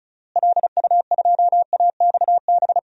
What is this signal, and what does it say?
Morse code, heard as a keyed tone, about 34 wpm, 700 Hz. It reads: LU2AXB
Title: DXCC-100 Call Signs - 6 Characters - Repeated 3x - 35wpm